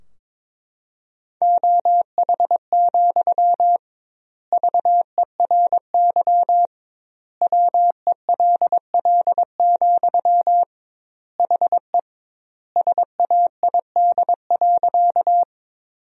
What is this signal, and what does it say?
Morse code, heard as a keyed tone, about 22 wpm, 700 Hz. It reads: OH, VERY WELL, HE SAID.
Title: 2x Word-Spacing - Sentences from Top 500 Words - 22wpm